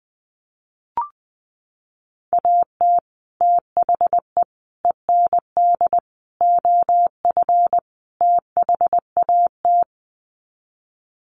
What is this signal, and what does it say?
Morse code, heard as a keyed tone, about 20 wpm, 700 Hz. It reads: AT THE END OF THAT
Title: Sets of 5 Words Encore2 20wpm